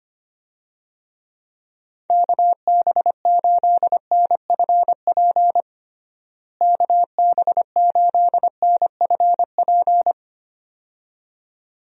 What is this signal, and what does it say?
Morse code, heard as a keyed tone, about 25 wpm, 700 Hz. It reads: KB8NFP KB8NFP
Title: USA Call Signs - 6 Characters - Repeated 2x - 25wpm